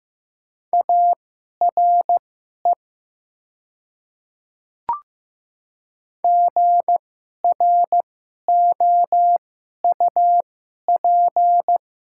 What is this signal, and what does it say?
Morse code, heard as a keyed tone, about 15 wpm, spaced as for 11 wpm, 700 Hz. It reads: ARE GROUP
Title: 2x Character Spacing - Top 300 Words - 15wpm